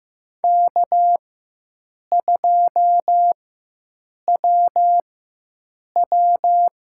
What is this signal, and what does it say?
Morse code, heard as a keyed tone, about 15 wpm, spaced as for 7 wpm, 700 Hz. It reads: K2WW
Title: USA Call Signs - 4 Characters - 4x Character Spacing - 15wpm